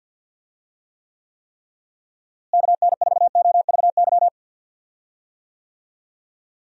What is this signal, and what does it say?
Morse code, heard as a keyed tone, about 52 wpm, 700 Hz. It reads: KN4OVY
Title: USA Call Signs - 6 Characters - Repeated 3x - 50wpm